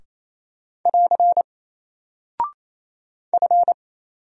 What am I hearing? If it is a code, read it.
Morse code, heard as a keyed tone, about 27 wpm, 700 Hz. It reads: <AR> F
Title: Single Character (Letter:Number:Punctuation:Prosign) 28wpm